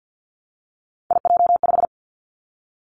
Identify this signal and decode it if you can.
Morse code, heard as a keyed tone, about 50 wpm, 700 Hz. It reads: IJ5
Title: Three Challenging Letter-Number Combinations 50wpm